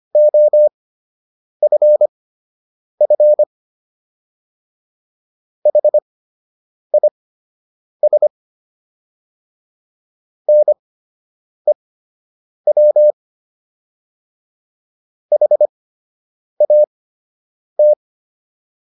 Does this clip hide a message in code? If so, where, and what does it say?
Yes: Morse code from 0.2 to 17.9 s, heard as a keyed tone, about 25 wpm, spaced as for 8 wpm, 600 Hz. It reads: OFF HIS NEW HAT